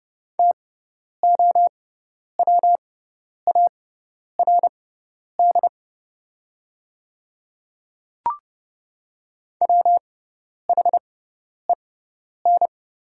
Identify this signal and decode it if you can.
Morse code, heard as a keyed tone, about 30 wpm, spaced as for 10 wpm, 700 Hz. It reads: TOWARD WHEN
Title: 6x Character Spacing - Top 500 Words - 30wpm